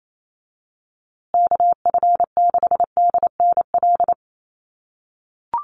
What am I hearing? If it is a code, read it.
Morse code, heard as a keyed tone, about 28 wpm, 700 Hz. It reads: KF6DNL
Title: USA Call Signs Repeated 3x 28wpm